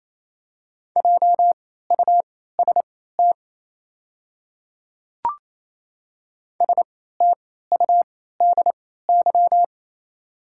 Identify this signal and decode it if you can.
Morse code, heard as a keyed tone, about 27 wpm, spaced as for 16 wpm, 700 Hz. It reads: JUST STUDY
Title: 3x Character Spacing - Top 300 Words - 28wpm